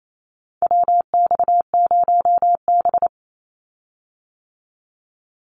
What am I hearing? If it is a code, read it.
Morse code, heard as a keyed tone, about 28 wpm, 700 Hz. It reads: WX0B
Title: Call Signs - 28wpm